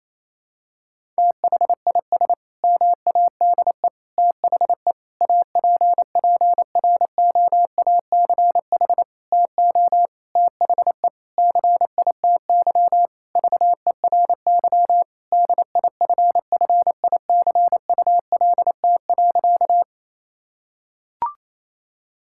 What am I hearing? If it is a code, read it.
Morse code, heard as a keyed tone, about 28 wpm, 700 Hz. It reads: THIS MADE THE APPROACH TO THE CITY VERY DIFFICULT.
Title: Sentences from Top 1000 Words - Review Part 6 - 28wpm